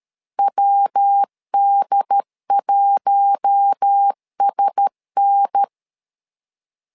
Morse code, heard as a keyed tone, about 13 wpm, 800 Hz. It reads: WD1SN
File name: LEZIONI / 4 Q1S 12wpm 1802